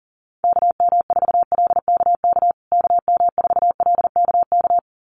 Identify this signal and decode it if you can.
Morse code, heard as a keyed tone, about 40 wpm, 700 Hz. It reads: KM4LKK KM4LKK
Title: USA Call Signs Repeated 2x 40wpm